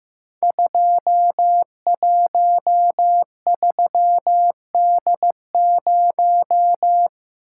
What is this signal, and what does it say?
Morse code, heard as a keyed tone, about 15 wpm, 700 Hz. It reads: 213D0
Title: Code Groups - 5 Letter:Numbers - 15wpm